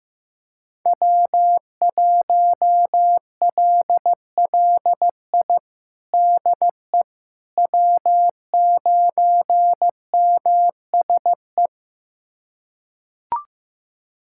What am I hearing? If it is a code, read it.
Morse code, heard as a keyed tone, about 15 wpm, 700 Hz. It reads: W1LLI DE W9MSE
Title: USA Callsign DE Callsign - 5 Characters - 15wpm